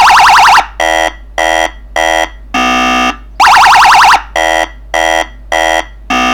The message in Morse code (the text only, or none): none